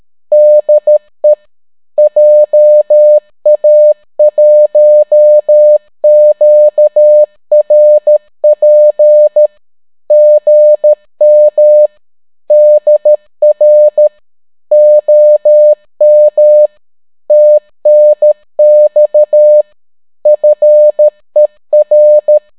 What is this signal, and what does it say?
Morse code, heard as a keyed tone, about 13 wpm, 600 Hz. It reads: DE JA1QRP GM DR OM TNX FER